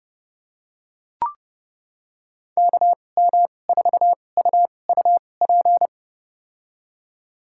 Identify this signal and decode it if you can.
Morse code, heard as a keyed tone, about 30 wpm, spaced as for 22 wpm, 700 Hz. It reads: KM4UUP
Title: USA Call Signs - 2x Character Spacing - 30wpm